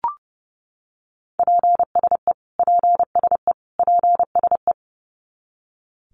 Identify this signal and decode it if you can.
Morse code, heard as a keyed tone, about 30 wpm, 700 Hz. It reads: PSE PSE PSE
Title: Abbreviations Repeated 3x - 30wpm